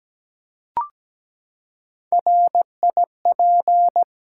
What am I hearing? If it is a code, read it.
Morse code, heard as a keyed tone, about 17 wpm, 700 Hz. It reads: RIP
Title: Three Letter Words - 17wpm